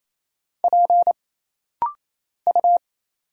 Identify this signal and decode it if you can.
Morse code, heard as a keyed tone, about 27 wpm, 700 Hz. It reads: P U
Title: Single Letter-Number Alt - Rapid-Fire 28wpm